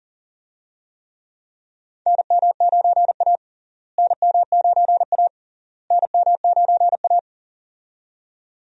Morse code, heard as a keyed tone, about 40 wpm, 700 Hz. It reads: NM9A NM9A NM9A
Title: USA Call Signs - 4 Characters - Repeated 3x - 40wpm